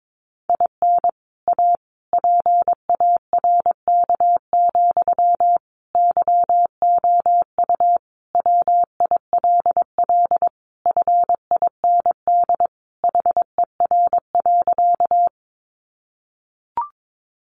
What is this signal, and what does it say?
Morse code, heard as a keyed tone, about 22 wpm, 700 Hz. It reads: IN A PARK, YOU WILL FIND HER.
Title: Sentences from Top 1300 Words 22wpm